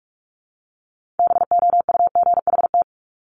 Morse code, heard as a keyed tone, about 46 wpm, 690 Hz. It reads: BOUGHT
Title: Multi-letter Phonemes Repeated 3x 45wpm